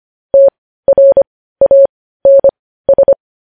Morse code, heard as a keyed tone, about 25 wpm, spaced as for 15 wpm, 550 Hz. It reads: TRANS
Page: news headlines in morse code at 15 wpm